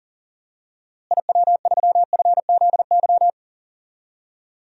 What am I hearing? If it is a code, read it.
Morse code, heard as a keyed tone, about 38 wpm, 700 Hz. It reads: IW3FZY
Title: DXCC-100 Call Signs - 6 Characters - Repeated 3x - 40wpm